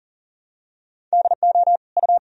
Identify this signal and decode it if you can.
Morse code, heard as a keyed tone, about 40 wpm, 700 Hz. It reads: DO U
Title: Sentences from Top 100 Words and Abbreviations 40wpm